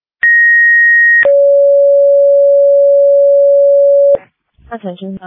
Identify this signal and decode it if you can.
Two-tone page, heard as a one-second tone, then a three-second tone, about 1900 then 560 Hz, with speech. It is heard from 0.2 to 4.2 s.